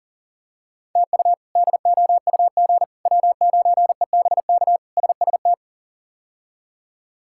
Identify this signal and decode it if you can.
Morse code, heard as a keyed tone, about 40 wpm, 700 Hz. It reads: TU DOUG W9EBK SST
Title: K1USN SST QSO Practice - 40wpm